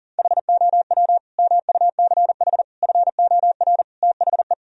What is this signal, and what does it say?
Morse code, heard as a keyed tone, about 40 wpm, 700 Hz. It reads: HOW MUCH FOR THE
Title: Sentences from Top 1000 Words - Review Part 3 - 40wpm